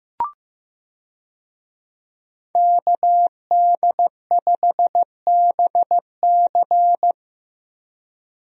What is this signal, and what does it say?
Morse code, heard as a keyed tone, about 15 wpm, 700 Hz. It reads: KD5BC
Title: USA Call Signs - 5 Characters - Repeated 2x - 15wpm